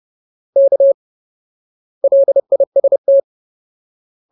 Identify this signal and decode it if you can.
Morse code, heard as a keyed tone, about 30 wpm, 550 Hz. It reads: K LIST
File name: Ultimate Word List / 30WPM Ultimate